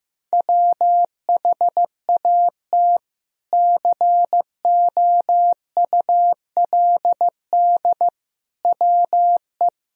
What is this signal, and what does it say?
Morse code, heard as a keyed tone, about 15 wpm, 700 Hz. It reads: WHAT COULD WE
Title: Sentences from Top 400 Words and Abbreviations 15wpm